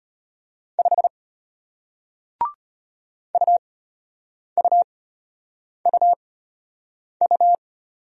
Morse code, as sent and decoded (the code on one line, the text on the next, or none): ..... / ..- ..- ..- ..-
5 UUUU